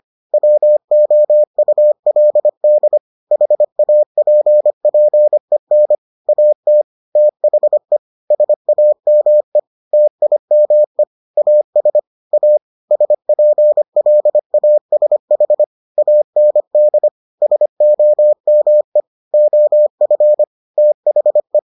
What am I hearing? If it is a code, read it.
Morse code, heard as a keyed tone, about 25 wpm, 600 Hz. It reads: WOULD HAPPEN AT THE SAME TIME AS A SPLASH AND SOME OF THE